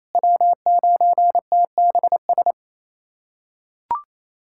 Morse code, heard as a keyed tone, about 28 wpm, 700 Hz. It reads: W9TBS
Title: USA Callsign DE Callsign - 5 Characters - 28wpm